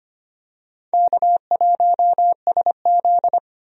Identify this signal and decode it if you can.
Morse code, heard as a keyed tone, about 25 wpm, 700 Hz. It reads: K1SZ
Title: USA Call Signs - 4 Characters - 25wpm